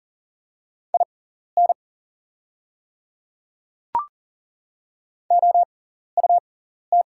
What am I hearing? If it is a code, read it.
Morse code, heard as a keyed tone, about 40 wpm, spaced as for 14 wpm, 700 Hz. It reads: IN OUT
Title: 6x Character Spacing - Top 100 Words - 40wpm